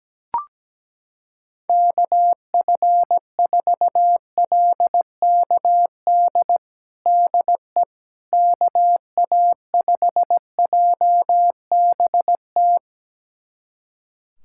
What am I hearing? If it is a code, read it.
Morse code, heard as a keyed tone, about 17 wpm, 700 Hz. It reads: KF4LKD DE KA5JBT
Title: USA Callsign DE Callsign - 6 Characters - 17wpm